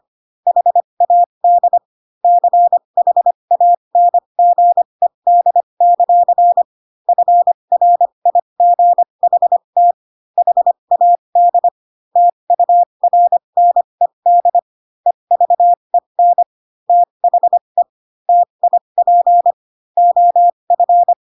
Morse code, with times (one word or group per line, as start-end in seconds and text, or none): 0.5-1.8 HAD
2.2-6.6 CHANGED;
7.1-9.9 FRIGHT
10.4-11.7 HAD
12.2-14.6 TURNED
15.1-16.4 EVEN
16.9-17.8 THE
18.3-19.5 TIP
20.0-21.1 OF